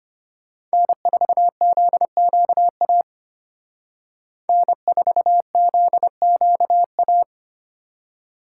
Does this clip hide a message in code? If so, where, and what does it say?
Yes: Morse code from 0.7 to 7.2 s, heard as a keyed tone, about 32 wpm, 700 Hz. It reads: N4ZQA N4ZQA